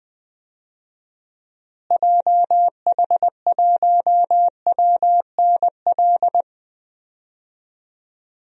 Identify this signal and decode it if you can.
Morse code, heard as a keyed tone, about 20 wpm, 700 Hz. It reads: JH1WNL
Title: DXCC-100 Call Signs - 6 Characters - Repeated 3x - 20wpm